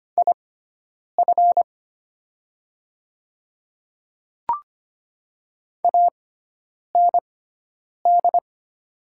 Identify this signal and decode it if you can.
Morse code, heard as a keyed tone, about 25 wpm, spaced as for 9 wpm, 700 Hz. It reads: IF AND
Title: 6x Character Spacing - Top 100 Words - 25wpm